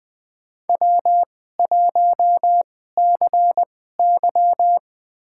Morse code, heard as a keyed tone, about 20 wpm, spaced as for 14 wpm, 700 Hz. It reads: W1CY
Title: USA Call Signs - 4 Characters - 2x Character Spacing - 20wpm